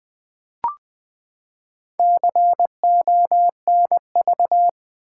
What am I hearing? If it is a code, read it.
Morse code, heard as a keyed tone, about 20 wpm, 700 Hz. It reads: CONV
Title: Build A Word - 4 Letters At A Time - 11-Letter Words - 20wpm